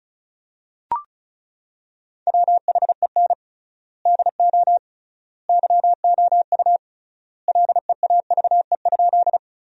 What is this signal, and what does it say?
Morse code, heard as a keyed tone, about 36 wpm, 700 Hz. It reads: WHEN DO YOU LEAVE?
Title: 3x Word-Spacing - Sentences from Top 400 Words - 35wpm